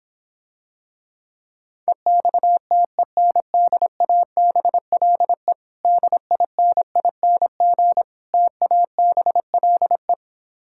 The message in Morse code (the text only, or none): EXTENDABLE DINING TABLE